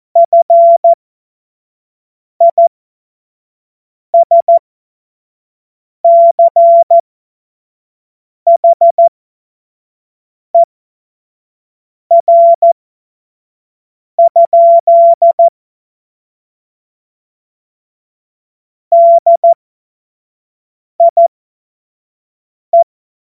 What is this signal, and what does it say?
Morse code, heard as a keyed tone, about 14 wpm, spaced as for 5 wpm, 680 Hz. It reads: FISCHER? DIE
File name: Morsekurs / Witze / Witz 04. Juli 2023 25 F70